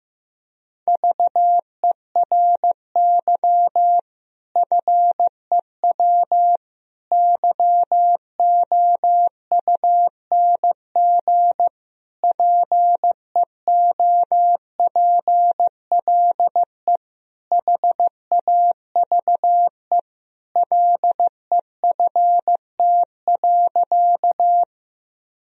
Morse code, heard as a keyed tone, about 15 wpm, 700 Hz. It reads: VERY FEW YOUNG PEOPLE HAVE LEFT.
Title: Sentences from Top 1000 Words - Review Part 1 - 15wpm